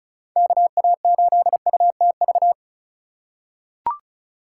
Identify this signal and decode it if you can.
Morse code, heard as a keyed tone, about 36 wpm, 700 Hz. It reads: KA8UTV
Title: USA Call Signs - 6 Characters - 4x Character Spacing - 35wpm